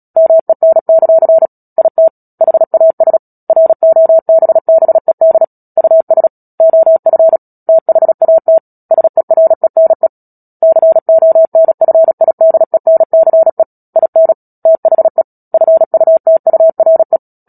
Morse code, heard as a keyed tone, about 37 wpm, 650 Hz. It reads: MEN; IT HAS ROBBED US OF THAT SERENE CONFIDENCE IN THE FUTURE
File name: Book Two / 10-Chapter 10 - Epilogue - 36 WPM